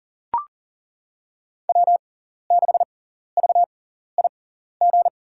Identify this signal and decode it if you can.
Morse code, heard as a keyed tone, about 40 wpm, spaced as for 14 wpm, 690 Hz. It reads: W6VIG
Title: USA Call Signs - 5 Characters - 6x Character Spacing - 40wpm